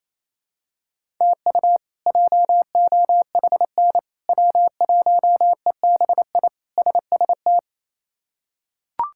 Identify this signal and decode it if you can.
Morse code, heard as a keyed tone, about 28 wpm, 700 Hz. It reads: TU JOHN W1EBI SST
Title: K1USN SST QSO Practice - 28wpm